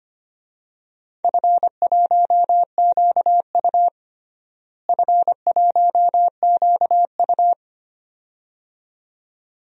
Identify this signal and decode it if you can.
Morse code, heard as a keyed tone, about 25 wpm, 700 Hz. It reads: F1QU F1QU
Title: DXCC-100 Call Signs - 4 Characters - Repeated 2x - 25wpm